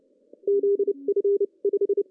Morse code, heard as a keyed tone, about 30 wpm, spaced as for 26 wpm, 370 Hz. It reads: ZF 5
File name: CW / cqww-cw-k5tr